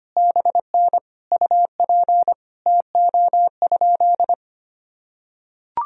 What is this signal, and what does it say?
Morse code, heard as a keyed tone, about 25 wpm, 700 Hz. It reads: BN UP TO?
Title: Sentences from Top 100 Words and Abbreviations 25wpm